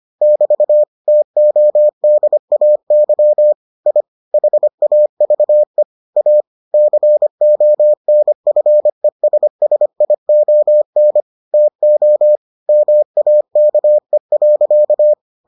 Morse code, heard as a keyed tone, about 25 wpm, 600 Hz. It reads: <BT> TODAY I HAVE A CONFESSION TO MAKE.